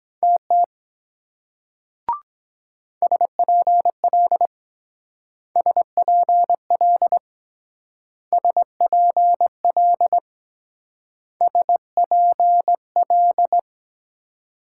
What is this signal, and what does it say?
Morse code, heard as a keyed tone, about 22 wpm, 700 Hz. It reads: TT SPL SPL SPL SPL